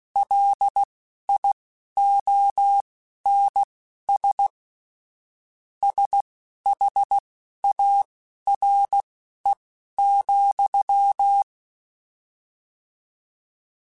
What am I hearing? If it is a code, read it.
Morse code, heard as a keyed tone, about 16 wpm, spaced as for 11 wpm, 800 Hz. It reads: LIONS SHARE,
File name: Misc / Misc-mp3s / Word Pairs @10